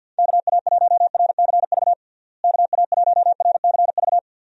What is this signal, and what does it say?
Morse code, heard as a keyed tone, about 50 wpm, 700 Hz. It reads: KA1RCV KA1RCV
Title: USA Call Signs - 6 Characters - Repeated 2x - 50wpm